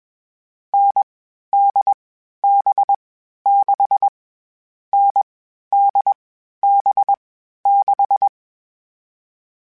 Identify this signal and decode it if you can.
Morse code, heard as a keyed tone, about 21 wpm, 800 Hz. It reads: N D B 6 N D B 6
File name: Variable Speed / Variable Speed-mp3s / GZ GZ7 sp @20